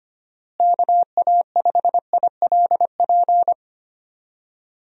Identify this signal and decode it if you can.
Morse code, heard as a keyed tone, about 25 wpm, 700 Hz. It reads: KA5ILP
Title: USA Call Signs - 6 Characters - Repeated 2x - 25wpm